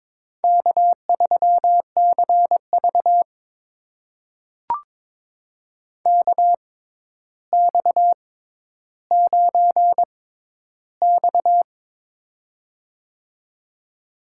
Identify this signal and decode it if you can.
Morse code, heard as a keyed tone, about 22 wpm, spaced as for 11 wpm, 700 Hz. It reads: K3CV KX9X
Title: USA Call Signs - 4 Characters - 6x Character Spacing - 22wpm